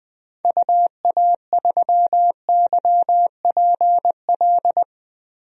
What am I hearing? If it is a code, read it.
Morse code, heard as a keyed tone, about 20 wpm, 700 Hz. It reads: UA3YPL
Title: DXCC-100 Call Signs - 6 Characters - 20wpm